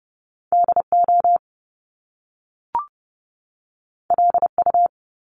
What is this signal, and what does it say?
Morse code, heard as a keyed tone, about 30 wpm, 700 Hz. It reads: DO LU
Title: Top Two Letter Combinations in Words 30wpm